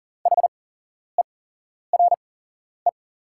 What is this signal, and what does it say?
Morse code, heard as a keyed tone, about 40 wpm, spaced as for 11 wpm, 690 Hz. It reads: HERE
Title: 8x Character Spacing - Top 300 Words - 40wpm